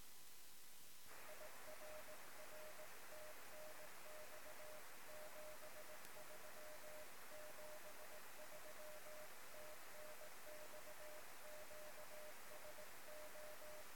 Morse code, heard as a keyed tone, about 19 wpm, 610 Hz, with noise. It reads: IRR TNX 73 73 TNX QSO
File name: sm7lcb / 1 oh2pu-p 13cm